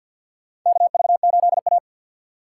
Morse code, heard as a keyed tone, about 53 wpm, 700 Hz. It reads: KV8A
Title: USA Call Signs - 4 Characters - 3x Character Spacing - 50wpm